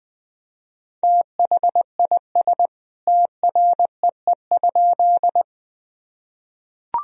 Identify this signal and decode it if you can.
Morse code, heard as a keyed tone, about 20 wpm, 700 Hz. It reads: THIS TREE?
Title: Sentences from Top 1000 Words - Review Part 1 - 20wpm